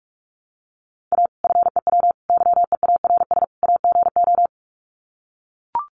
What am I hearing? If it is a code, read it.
Morse code, heard as a keyed tone, about 44 wpm, 700 Hz. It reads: A FEW YEARS AGO
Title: Sets of 4 Words 45wpm